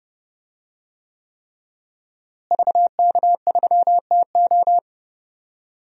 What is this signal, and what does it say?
Morse code, heard as a keyed tone, about 30 wpm, 700 Hz. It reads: VK3TO
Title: DXCC-100 Call Signs - 5 Characters - Repeated 3x - 30wpm